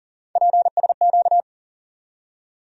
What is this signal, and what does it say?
Morse code, heard as a keyed tone, about 40 wpm, 700 Hz. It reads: PSQ
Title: USA Call Sign Postfixes - 3 Letters 40wpm